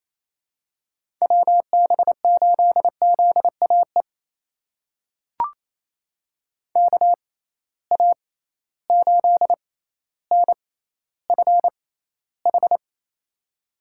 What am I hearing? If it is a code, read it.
Morse code, heard as a keyed tone, about 27 wpm, spaced as for 14 wpm, 700 Hz. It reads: WB8ZAE KA8NFH
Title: USA Call Signs - 6 Characters - 6x Character Spacing - 28wpm